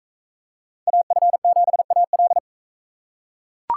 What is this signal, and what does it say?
Morse code, heard as a keyed tone, about 38 wpm, 700 Hz. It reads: AF7AL